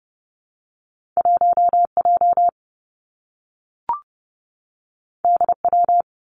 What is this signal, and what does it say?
Morse code, heard as a keyed tone, about 30 wpm, 700 Hz. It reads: 1 J D W
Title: Two Challenging Letter-Number Combinations 30wpm